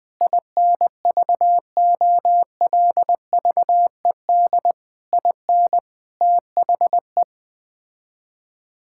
Morse code, heard as a keyed tone, about 20 wpm, 700 Hz. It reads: INVOLVED IN THE
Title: Sets of 6 Words - 20wpm